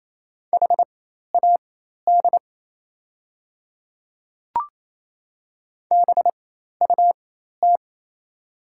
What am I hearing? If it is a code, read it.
Morse code, heard as a keyed tone, about 28 wpm, spaced as for 13 wpm, 690 Hz. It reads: HAD BUT
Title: 4x Character Spacing - Top 300 Words - 28wpm